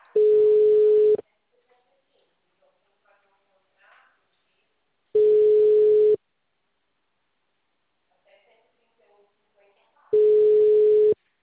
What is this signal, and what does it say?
Ringback tone (Europe, 425 Hz), with speech.